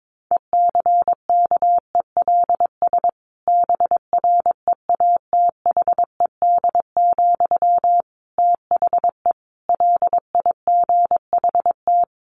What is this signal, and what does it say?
Morse code, heard as a keyed tone, about 22 wpm, 700 Hz. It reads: ECKELS BREATHED, THE LIGHT
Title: A Sound Of Thunder 22wpm